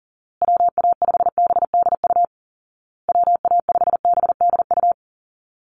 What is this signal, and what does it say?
Morse code, heard as a keyed tone, about 37 wpm, 700 Hz. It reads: WA5BDU WA5BDU
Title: Call Signs Repeated 2x - 40wpm